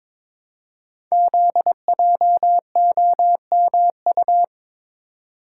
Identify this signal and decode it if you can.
Morse code, heard as a keyed tone, about 22 wpm, 700 Hz. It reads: ZJOMU
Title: Code Groups - 5 Letters - 22wpm